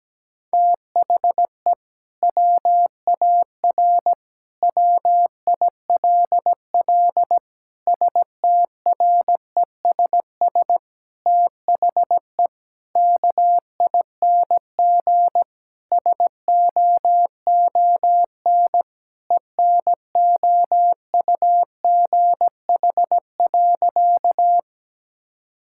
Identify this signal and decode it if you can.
Morse code, heard as a keyed tone, about 17 wpm, 700 Hz. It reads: THE WAR WILL STRESS THE KING SOON ENOUGH.
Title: Sentences from Top 1000 Words - Review Part 6 - 17wpm